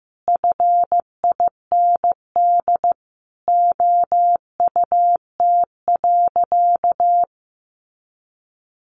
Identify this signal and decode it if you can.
Morse code, heard as a keyed tone, about 15 wpm, 700 Hz. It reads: FIND OUT.
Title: Sentences from Top 200 Words 15wpm